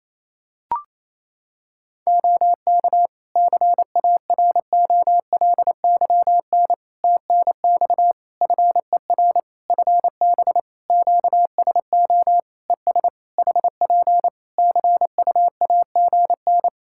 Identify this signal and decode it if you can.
Morse code, heard as a keyed tone, about 28 wpm, 700 Hz. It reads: OK CAROLYN TNX FER FB QSO ES HP CUAGN
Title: Practice QSO - Exchange 4 - Ending - 28wpm